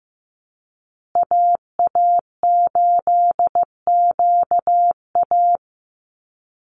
Morse code, heard as a keyed tone, about 15 wpm, 700 Hz. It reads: AA8QA
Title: USA Call Signs Repeated 3x 15wpm